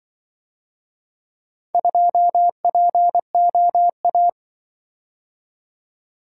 Morse code, heard as a keyed tone, about 23 wpm, 700 Hz. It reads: 2POA
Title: Speed-Racing - Four Letter-Number Combinations - 20wpm